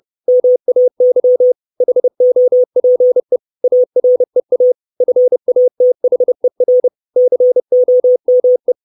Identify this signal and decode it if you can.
Morse code, heard as a keyed tone, about 30 wpm, 500 Hz. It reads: MAY HOPE AREA FATHER COME
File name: 632 Words 30WPM